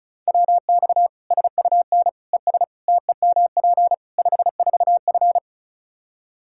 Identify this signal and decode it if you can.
Morse code, heard as a keyed tone, about 36 wpm, 700 Hz. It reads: WX SUN ES TEMP 54F
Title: Practice QSO - Exchange 2 - Info - 35wpm